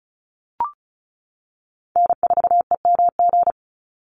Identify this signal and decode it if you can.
Morse code, heard as a keyed tone, about 34 wpm, 700 Hz. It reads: N4EMG
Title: NA SOTA Chaser Callsigns 35wpm